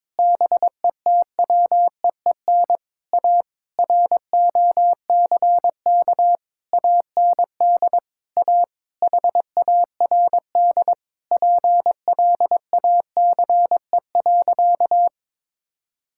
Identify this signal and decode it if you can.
Morse code, heard as a keyed tone, about 21 wpm, 700 Hz. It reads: BETWEEN A ROCK AND A HARD PLACE.
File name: Sentences with 7-Word Phrases - 22wpm